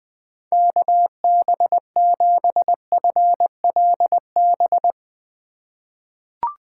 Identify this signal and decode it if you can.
Morse code, heard as a keyed tone, about 20 wpm, 700 Hz. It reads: KB7FLB
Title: USA Call Signs - 2x Character Spacing - 20wpm